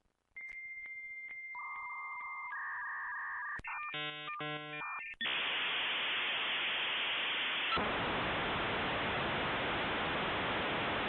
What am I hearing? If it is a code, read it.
Dial-up modem handshake.